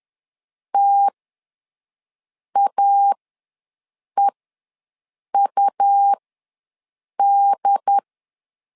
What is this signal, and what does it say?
Morse code, heard as a keyed tone, about 10 wpm, spaced as for 6 wpm, 800 Hz. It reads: TAEUD
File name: LEZIONI / 4 Q ws 10wpm 2306